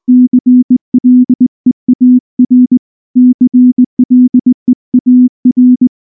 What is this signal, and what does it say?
Morse code, heard as a keyed tone, about 20 wpm, 250 Hz. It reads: CLEAR CLEAR
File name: morse / morse-training-20WPM-4